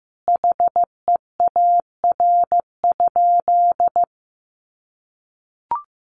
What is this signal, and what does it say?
Morse code, heard as a keyed tone, about 15 wpm, 700 Hz. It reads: HEAR?
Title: Sentences from Top 300 Words 15wpm